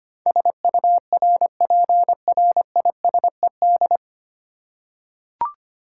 Morse code, heard as a keyed tone, about 25 wpm, 700 Hz. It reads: SURPRISED
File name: practice / 25wpm / 9 Letter Words - First 3 Letters Given - 25wpm